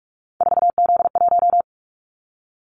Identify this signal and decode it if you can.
Morse code, heard as a keyed tone, about 44 wpm, 700 Hz. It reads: 4Z1
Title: Callsign Prefixes for Top 50 DXCC - 45wpm